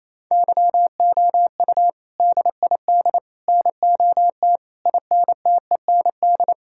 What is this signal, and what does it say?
Morse code, heard as a keyed tone, about 28 wpm, 700 Hz. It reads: YOU DID NOT INTEND
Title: Sentences from Top 1600 Words 28wpm